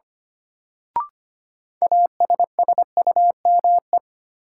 Morse code, heard as a keyed tone, about 25 wpm, 700 Hz. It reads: ASSUME
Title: Sentences from Top 1500 Words - New Word Drill - 25wpm